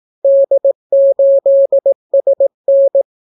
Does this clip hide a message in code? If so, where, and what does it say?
Yes: Morse code from 0.2 to 3.0 s, heard as a keyed tone, about 18 wpm, 550 Hz. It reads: D8SN